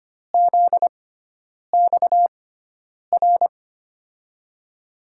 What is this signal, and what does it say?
Morse code, heard as a keyed tone, about 25 wpm, spaced as for 9 wpm, 700 Hz. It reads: ZXR
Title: USA Call Signs - 6x Character Spacing - 25wpm